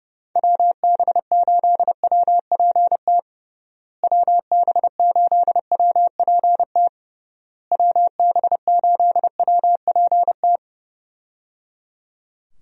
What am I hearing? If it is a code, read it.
Morse code, heard as a keyed tone, about 30 wpm, 700 Hz. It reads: WB8WPT WB8WPT WB8WPT
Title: USA Call Signs - 6 Characters - Repeated 3x - 30wpm